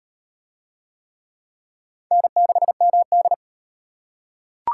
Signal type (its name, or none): Morse code